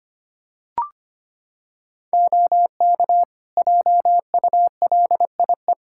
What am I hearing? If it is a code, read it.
Morse code, heard as a keyed tone, about 25 wpm, 700 Hz. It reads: OK JULIE
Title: Practice QSO - Exchange 3 - More Info - 25wpm